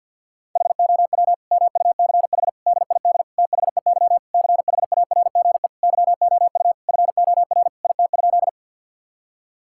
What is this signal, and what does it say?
Morse code, heard as a keyed tone, about 50 wpm, 700 Hz. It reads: HOW MUCH DID THEY CHARGE YOU FOR IT?
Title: Sentences from Top 1000 Words - Review Part 6 - 50wpm